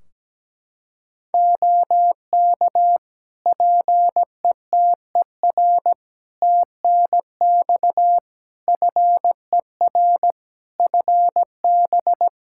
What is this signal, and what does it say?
Morse code, heard as a keyed tone, about 17 wpm, 700 Hz. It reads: OK PETER TNX FER FB
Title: K1USN SST QSO Practice 17wpm